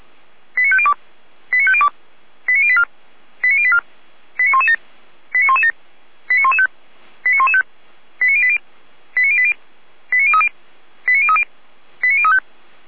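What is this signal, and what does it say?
Five-tone selective call (ZVEI), heard as quick runs of five tones, tones of about 70 ms. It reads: 89502 89502 89084 89084 89117 89117 89115 89115 89080 89080 89030 89030 89035
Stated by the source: ZVEI five-tone selective call 89120 received with an SDR.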